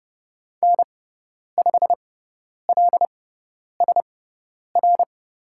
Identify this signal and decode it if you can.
Morse code, heard as a keyed tone, about 30 wpm, spaced as for 10 wpm, 700 Hz. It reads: N5LSR
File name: Lesson 16g - Letter c - Call Signs 30wpm